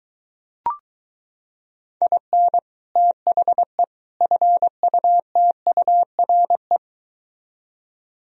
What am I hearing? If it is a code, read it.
Morse code, heard as a keyed tone, about 24 wpm, 700 Hz. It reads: IN THE FUTURE